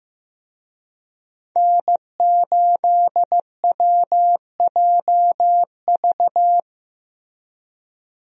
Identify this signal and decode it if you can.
Morse code, heard as a keyed tone, about 15 wpm, 700 Hz. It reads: N8WJV
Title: USA Call Signs - 5 Characters - Repeated 3x - 15wpm